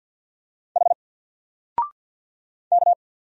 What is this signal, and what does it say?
Morse code, heard as a keyed tone, about 50 wpm, 700 Hz. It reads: H K
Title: Single Letter-Number 50wpm